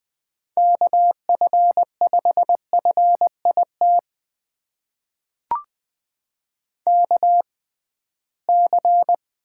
Morse code, heard as a keyed tone, about 20 wpm, spaced as for 15 wpm, 700 Hz. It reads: KF5FIT KC
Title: USA Call Signs - 6 Characters - 6x Character Spacing - 20wpm